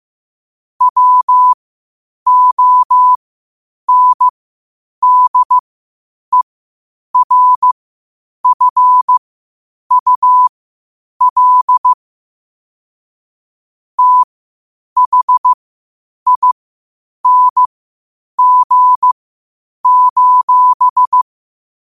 Morse code, heard as a keyed tone, about 15 wpm, spaced as for 8 wpm, 1000 Hz. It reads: WONDERFUL THING:
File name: 8wpm